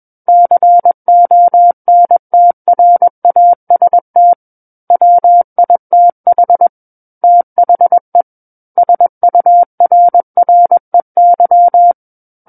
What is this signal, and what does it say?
Morse code, heard as a keyed tone, about 21 wpm, 700 Hz. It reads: CONTRAST WITH THE SURREY
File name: Book One / 12-Chapter 12 - 21 WPM